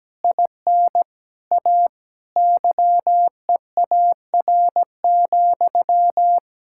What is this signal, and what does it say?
Morse code, heard as a keyed tone, about 17 wpm, 700 Hz. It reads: IN A YEAR,
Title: Sentences from Top 1600 Words 17wpm